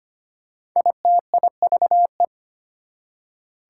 Morse code, heard as a keyed tone, about 25 wpm, 700 Hz. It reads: ITIVE
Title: Prefixes and Suffixes 25wpm